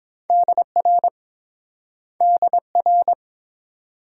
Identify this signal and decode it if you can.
Morse code, heard as a keyed tone, about 28 wpm, 700 Hz. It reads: DR DR